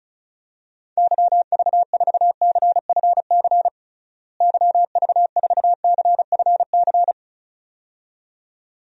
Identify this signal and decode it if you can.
Morse code, heard as a keyed tone, about 34 wpm, 700 Hz. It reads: YV4CFC YV4CFC
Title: DXCC-100 Call Signs - 6 Characters - Repeated 2x - 35wpm